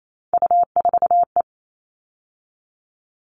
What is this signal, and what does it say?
Morse code, heard as a keyed tone, about 28 wpm, 700 Hz. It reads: U4E
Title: All Cut Numbers - 3-Digit 28wpm